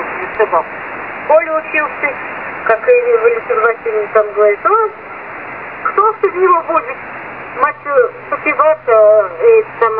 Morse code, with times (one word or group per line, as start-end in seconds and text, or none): none